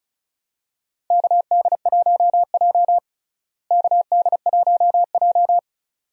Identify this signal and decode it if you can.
Morse code, heard as a keyed tone, about 34 wpm, 700 Hz. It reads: KD1J KD1J
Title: USA Call Signs - 4 Characters - Repeated 2x - 35wpm